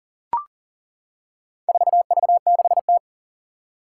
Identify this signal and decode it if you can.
Morse code, heard as a keyed tone, about 40 wpm, 700 Hz. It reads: 4V6T
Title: Four Challenging Letter-Number Combinations 40wpm